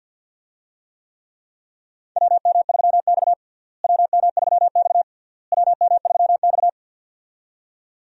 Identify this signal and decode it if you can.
Morse code, heard as a keyed tone, about 52 wpm, 700 Hz. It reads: WM3X WM3X WM3X
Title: USA Call Signs - 4 Characters - Repeated 3x - 50wpm